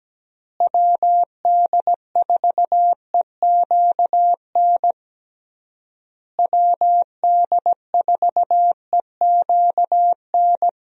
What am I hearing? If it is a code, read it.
Morse code, heard as a keyed tone, about 17 wpm, 700 Hz. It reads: WD4EQN WD4EQN
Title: USA Call Signs - 6 Characters - Repeated 3x - 17wpm